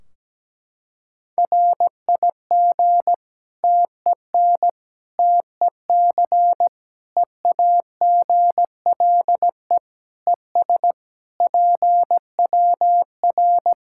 Morse code, heard as a keyed tone, about 17 wpm, 700 Hz. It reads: RIG TEN TEC EAGLE ES PWR
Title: K1USN SST QSO Practice 17wpm